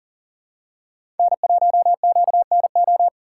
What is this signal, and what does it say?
Morse code, heard as a keyed tone, about 40 wpm, 700 Hz. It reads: N1QNO
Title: USA Call Signs - 4x Character Spacing - 40wpm